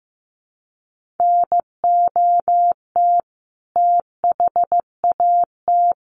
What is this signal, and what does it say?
Morse code, heard as a keyed tone, about 15 wpm, 700 Hz. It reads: NOT THAT
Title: Sentences from Top 1400 Words 15wpm